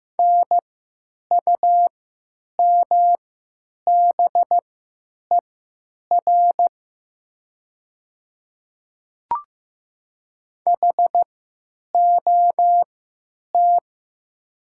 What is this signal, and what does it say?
Morse code, heard as a keyed tone, about 15 wpm, spaced as for 9 wpm, 700 Hz. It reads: NUMBER HOT
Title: 3x Character Spacing - Top 100 Words - 15wpm